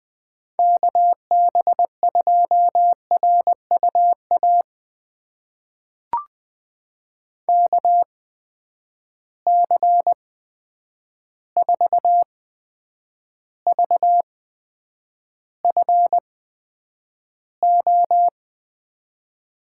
Morse code, heard as a keyed tone, about 20 wpm, spaced as for 9 wpm, 700 Hz. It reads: KB2RUA KC4VFO
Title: USA Call Signs - 8x Character Spacing - 20wpm